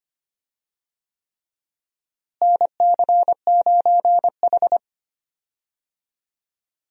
Morse code, heard as a keyed tone, about 25 wpm, 700 Hz. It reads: NC9H
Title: USA Call Signs - 4 Characters - Repeated 3x - 25wpm